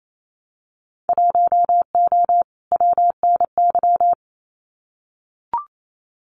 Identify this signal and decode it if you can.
Morse code, heard as a keyed tone, about 28 wpm, 700 Hz. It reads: 1O WNY
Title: Winter Field Day Exchanges Encore 28wpm